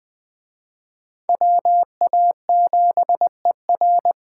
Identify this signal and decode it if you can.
Morse code, heard as a keyed tone, about 20 wpm, 700 Hz. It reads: WA7ER
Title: Speed Racing - USA Call Signs 17wpm